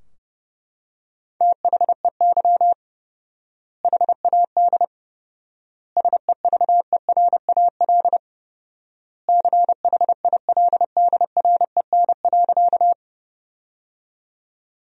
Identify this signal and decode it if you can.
Morse code, heard as a keyed tone, about 30 wpm, 700 Hz. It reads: THEY HAD SEVERAL CHILDREN.
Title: 4x Word-Spacing - Sentences from Top 400 Words - 30wpm